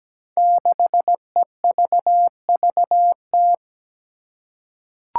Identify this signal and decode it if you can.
Morse code, heard as a keyed tone, about 17 wpm, 700 Hz. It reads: 6EVVT
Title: Five Challenging Letter-Number Combinations 17wpm